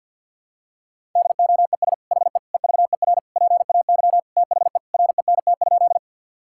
Morse code, heard as a keyed tone, about 50 wpm, 700 Hz. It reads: DOES HE EVER PAY THE RENT?
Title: Sentences from Top 1000 Words - Review Part 5 - 50wpm